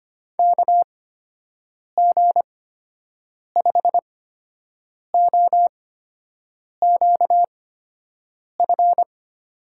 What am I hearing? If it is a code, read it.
Morse code, heard as a keyed tone, about 25 wpm, spaced as for 7 wpm, 700 Hz. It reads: KG5OQF
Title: USA Call Signs - 6 Characters - 8x Character Spacing - 25wpm